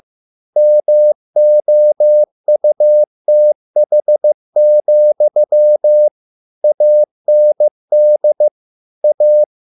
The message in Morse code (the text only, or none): MOUTH, AND A